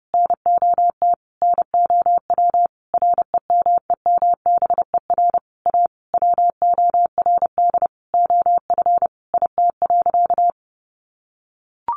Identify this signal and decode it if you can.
Morse code, heard as a keyed tone, about 30 wpm, 700 Hz. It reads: NOT NOW REMEMBER A WORD OF IT.
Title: Sentences from Top 1200 Words 30wpm